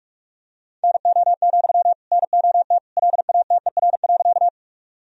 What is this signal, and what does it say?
Morse code, heard as a keyed tone, about 46 wpm, 700 Hz. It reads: NO, NOT LATER.
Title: Sentences from Top 1000 Words - Review Part 4 - 45wpm